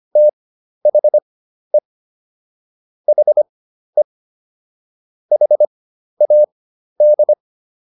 Morse code, heard as a keyed tone, about 25 wpm, spaced as for 12 wpm, 600 Hz. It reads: THE HE HAD